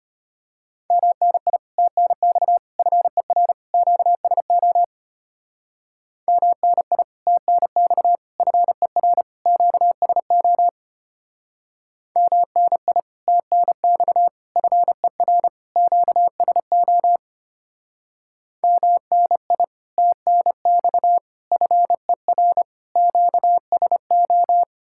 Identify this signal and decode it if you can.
Morse code, heard as a keyed tone, about 29 wpm, 700 Hz. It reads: MNI TNX FER QSO MNI TNX FER QSO MNI TNX FER QSO MNI TNX FER QSO
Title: Speed-Racing Common QSO Phrases - 25wpm